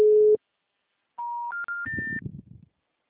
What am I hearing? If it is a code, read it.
Special information tone.